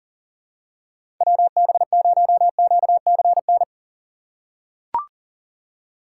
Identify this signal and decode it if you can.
Morse code, heard as a keyed tone, about 40 wpm, 700 Hz. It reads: WB0QCN
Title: USA Call Signs - 6 Characters - 3x Character Spacing - 40wpm